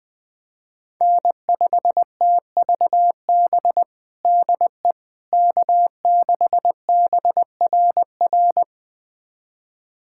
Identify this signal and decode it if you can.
Morse code, heard as a keyed tone, about 21 wpm, 700 Hz. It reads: N5TVB DE K6BRR
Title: USA Callsign DE Callsign - 5 Characters - 20wpm